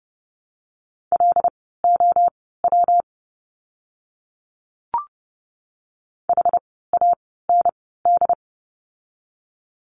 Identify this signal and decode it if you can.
Morse code, heard as a keyed tone, about 30 wpm, spaced as for 17 wpm, 700 Hz. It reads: LOW HAND
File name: course / 30:10wpm / 115 Lesson 19f - Letter w - Words 30wpm